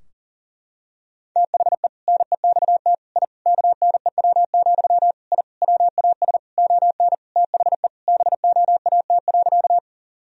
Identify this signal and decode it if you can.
Morse code, heard as a keyed tone, about 40 wpm, 700 Hz. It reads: THE NEXT I KNEW, I WAS ON THE BOAT.
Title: Sentences from Top 1000 Words - Review Part 3 - 40wpm